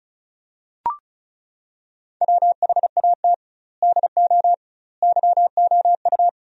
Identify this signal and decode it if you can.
Morse code, heard as a keyed tone, about 36 wpm, 700 Hz. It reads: WHAT DO YOU
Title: 2x Word-Spacing - Sentences from Top 100 Words - 35wpm